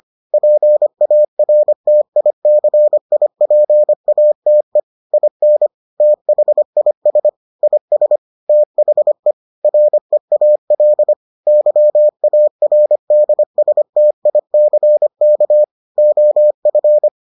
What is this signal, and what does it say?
Morse code, heard as a keyed tone, about 25 wpm, 600 Hz. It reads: PARTICIPATE IN THIS IS THE REAL YARDSTICK OF